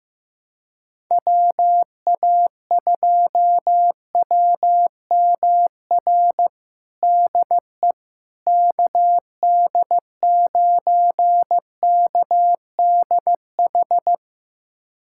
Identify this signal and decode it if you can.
Morse code, heard as a keyed tone, about 15 wpm, 700 Hz. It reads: WA2WMR DE KD9KDH
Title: USA Callsign DE Callsign - 6 Characters - 15wpm